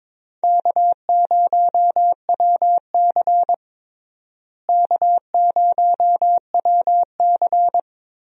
Morse code, heard as a keyed tone, about 22 wpm, 700 Hz. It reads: K0WC K0WC
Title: USA Call Signs - 4 Characters - Repeated 3x - 22wpm